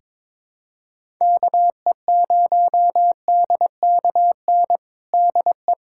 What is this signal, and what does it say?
Morse code, heard as a keyed tone, about 22 wpm, 700 Hz. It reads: KE0DKN DE
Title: USA Callsign DE Callsign - 6 Characters - 22wpm